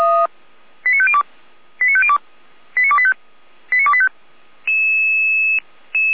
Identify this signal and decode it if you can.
Five-tone selective call (ZVEI), heard as quick runs of five tones, tones of about 80 ms. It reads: 89502 89502 89285 89285
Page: /455/SDR/ZVEI/89262/